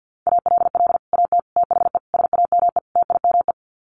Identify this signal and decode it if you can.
Morse code, heard as a keyed tone, about 50 wpm, 700 Hz. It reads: ALL AT THE SAME TIME
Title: Sets of 5 Words Encore2 50wpm